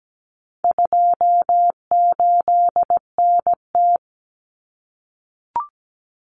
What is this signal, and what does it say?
Morse code, heard as a keyed tone, about 17 wpm, 700 Hz. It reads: 28NT
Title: Common Cut Numbers - 4-Digit 17wpm